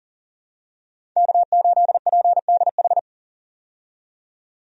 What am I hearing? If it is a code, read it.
Morse code, heard as a keyed tone, about 40 wpm, 700 Hz. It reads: K8PDH
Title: USA Call Signs - 5 Characters - 40wpm